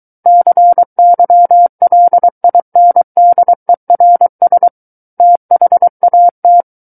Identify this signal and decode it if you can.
Morse code, heard as a keyed tone, about 23 wpm, 700 Hz. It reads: CYLINDERS THAT
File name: Book One / 14-Chapter 14 - 23 WPM